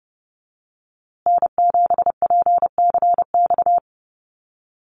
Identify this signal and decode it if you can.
Morse code, heard as a keyed tone, about 30 wpm, 700 Hz. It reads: N7PCX
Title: USA Call Signs Repeated 3x 30wpm